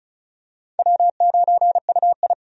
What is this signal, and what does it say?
Morse code, heard as a keyed tone, about 36 wpm, 700 Hz. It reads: W9UI
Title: USA Call Signs - 4 Characters - 35wpm